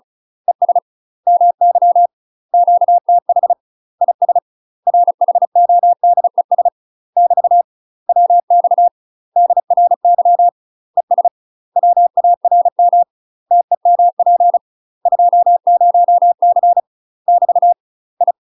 Morse code, heard as a keyed tone, about 35 wpm, 700 Hz. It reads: ES MY QTH IS RHODES <BT> WX DRY ES WARM TEMP 20C <BT> I
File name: UK MASTER QSOs TRAINING / 35WPM / UK MASTER QSOs 0012